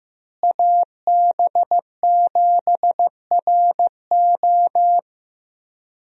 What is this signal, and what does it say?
Morse code, heard as a keyed tone, about 15 wpm, 700 Hz. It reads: AB7RO
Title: USA Call Signs - 5 Characters - Repeated 3x - 15wpm